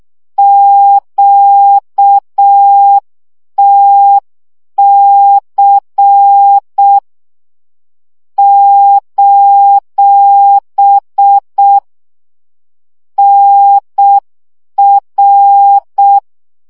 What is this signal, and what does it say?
Morse code, heard as a keyed tone, about 6 wpm, 800 Hz. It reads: QTC : NR
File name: morze vizsga 2011-3 5